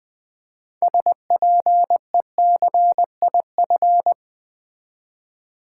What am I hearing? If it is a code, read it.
Morse code, heard as a keyed tone, about 20 wpm, 700 Hz. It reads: SPECIF
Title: Build A Word - 6 Letters At A Time - 12-Letter Words - 20wpm